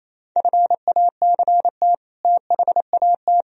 Morse code, heard as a keyed tone, about 36 wpm, 690 Hz. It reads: FACT THAT